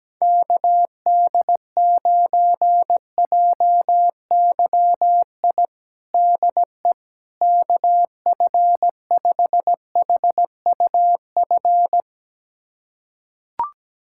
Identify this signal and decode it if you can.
Morse code, heard as a keyed tone, about 17 wpm, 700 Hz. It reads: KD9JYI DE KF5HUF
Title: USA Callsign DE Callsign 17wpm